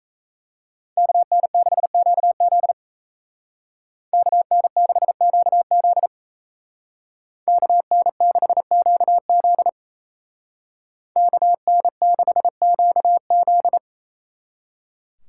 Morse code, heard as a keyed tone, about 34 wpm, 700 Hz. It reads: KN6QZ KN6QZ KN6QZ KN6QZ
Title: Speed Racing - USA 2x2 Call Signs 28wpm